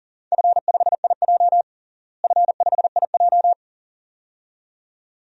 Morse code, heard as a keyed tone, about 40 wpm, 700 Hz. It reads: F5IJ F5IJ
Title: DXCC-100 Call Signs - 4 Characters - Repeated 2x - 40wpm